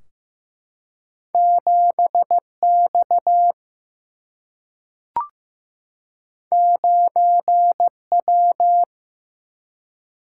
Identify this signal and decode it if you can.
Morse code, heard as a keyed tone, about 15 wpm, 700 Hz. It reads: 7 X 9 W
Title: Single-Letter Single-Number 15wpm